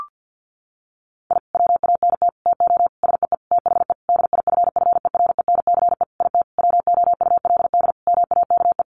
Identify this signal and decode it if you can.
Morse code, heard as a keyed tone, about 50 wpm, 700 Hz. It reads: I WANT TO SEE THE DIFFERENCE IT WOULD MAKE
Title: Sentences from Top 1300 Words 50wpm